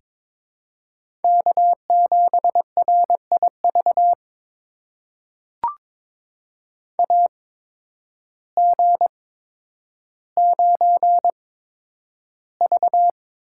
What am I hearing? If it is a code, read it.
Morse code, heard as a keyed tone, about 22 wpm, spaced as for 10 wpm, 700 Hz. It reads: K7RIV AG9V
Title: USA Call Signs - 8x Character Spacing - 22wpm